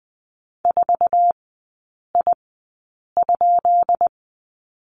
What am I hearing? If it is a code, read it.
Morse code, heard as a keyed tone, about 19 wpm, spaced as for 8 wpm, 700 Hz. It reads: 4I?
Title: 159 Lesson 26e - Letter g - Three Characters 20wpm